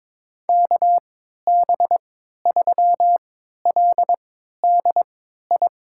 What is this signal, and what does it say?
Morse code, heard as a keyed tone, about 22 wpm, spaced as for 12 wpm, 700 Hz. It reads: KB3LDI
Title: USA Call Signs - 6 Characters - 3x Character Spacing - 22wpm